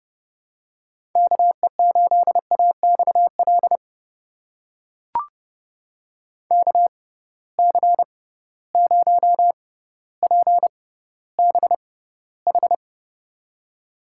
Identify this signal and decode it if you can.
Morse code, heard as a keyed tone, about 32 wpm, spaced as for 16 wpm, 700 Hz. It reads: KE8AXL KC0PBH
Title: USA Call Signs - 6 Characters - 6x Character Spacing - 30wpm